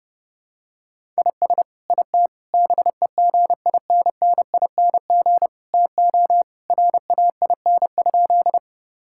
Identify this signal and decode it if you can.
Morse code, heard as a keyed tone, about 30 wpm, 700 Hz. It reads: IS IT BEGINNING TO RAIN?
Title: Sentences from Top 1000 Words - Review Part 5 - 30wpm